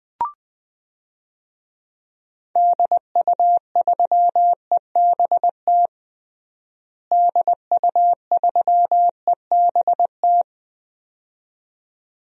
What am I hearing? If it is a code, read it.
Morse code, heard as a keyed tone, about 20 wpm, 700 Hz. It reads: DU3EBT DU3EBT
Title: DXCC-100 Call Signs - 6 Characters - Repeated 2x - 20wpm